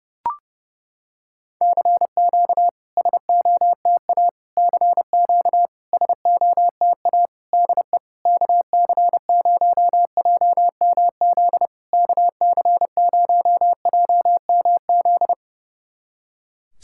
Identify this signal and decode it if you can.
Morse code, heard as a keyed tone, about 32 wpm, 700 Hz. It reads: CQ SOTA CQ SOTA DE KC0JMZ KC0JMZ
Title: SOTA QSO Practice 30wpm